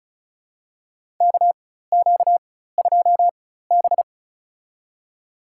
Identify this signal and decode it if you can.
Morse code, heard as a keyed tone, about 36 wpm, spaced as for 16 wpm, 700 Hz. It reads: KQ2B
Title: USA Call Signs - 4 Characters - 4x Character Spacing - 35wpm